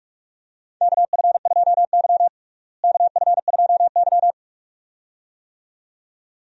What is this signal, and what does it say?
Morse code, heard as a keyed tone, about 43 wpm, 700 Hz. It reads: KF2Y KF2Y
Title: USA Call Signs - 4 Characters - Repeated 2x - 45wpm